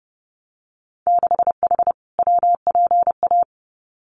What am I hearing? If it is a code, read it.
Morse code, heard as a keyed tone, about 30 wpm, 700 Hz. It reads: 6H WPA
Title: Winter Field Day Exchanges Encore 30wpm